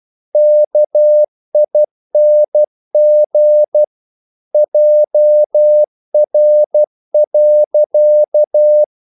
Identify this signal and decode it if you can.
Morse code, heard as a keyed tone, about 12 wpm, 600 Hz. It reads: KING JR.
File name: CW / Practice / EnglishQuotations0112